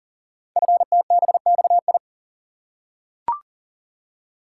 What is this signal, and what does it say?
Morse code, heard as a keyed tone, about 42 wpm, 700 Hz. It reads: FTBXI
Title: Code Groups - 5 Letters - 40wpm